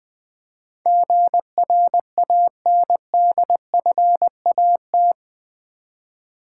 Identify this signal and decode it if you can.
Morse code, heard as a keyed tone, about 20 wpm, 700 Hz. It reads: GRANDFAT
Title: Build A Word - 4 Letters At A Time - 11-Letter Words - 20wpm